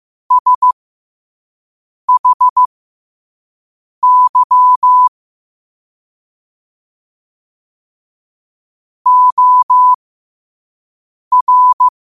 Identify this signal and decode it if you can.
Morse code, heard as a keyed tone, about 15 wpm, spaced as for 5 wpm, 1000 Hz. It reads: SHY OR
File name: qotdfiles / 5wpm